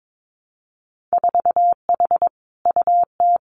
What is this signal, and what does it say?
Morse code, heard as a keyed tone, about 22 wpm, 700 Hz. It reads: 4H UT
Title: Winter Field Day Exchanges Encore 22wpm